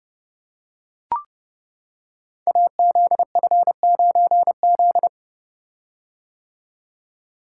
Morse code, heard as a keyed tone, about 30 wpm, 700 Hz. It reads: AZF9Z
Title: Code Groups - 5 Letter:Numbers - 30wpm